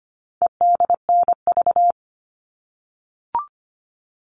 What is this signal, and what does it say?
Morse code, heard as a keyed tone, about 25 wpm, 700 Hz. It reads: EDNV
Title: All Cut Numbers - 4-Digit 25wpm